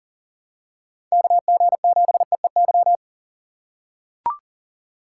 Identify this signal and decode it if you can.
Morse code, heard as a keyed tone, about 39 wpm, 700 Hz. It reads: KG7EEY
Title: USA Call Signs - 6 Characters - 2x Character Spacing - 40wpm